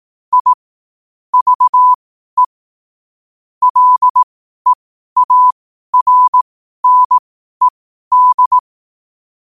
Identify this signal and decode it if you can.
Morse code, heard as a keyed tone, about 18 wpm, spaced as for 12 wpm, 1000 Hz. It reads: I VE LEARNED
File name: qotdfiles / 12wpm